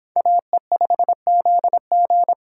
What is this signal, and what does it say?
Morse code, heard as a keyed tone, about 29 wpm, 700 Hz. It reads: AE5ZG